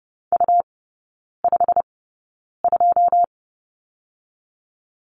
Morse code, heard as a keyed tone, about 30 wpm, spaced as for 9 wpm, 700 Hz. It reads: U52
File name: Lesson 18e - Letter m - Three Characters 30wpm